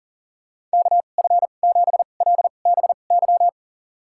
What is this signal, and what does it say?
Morse code, heard as a keyed tone, about 40 wpm, spaced as for 29 wpm, 690 Hz. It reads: KF7LBY
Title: USA Call Signs - 6 Characters - 2x Character Spacing - 40wpm